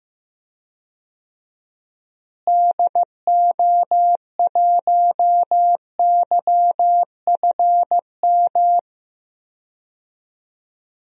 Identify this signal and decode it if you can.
Morse code, heard as a keyed tone, about 15 wpm, 700 Hz. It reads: DO1YFM
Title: DXCC-100 Call Signs - 6 Characters - Repeated 2x - 15wpm